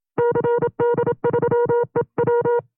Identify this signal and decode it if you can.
Morse code, heard as a keyed tone, about 27 wpm, 480 Hz. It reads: CD3EW
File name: CW-practice/25WPM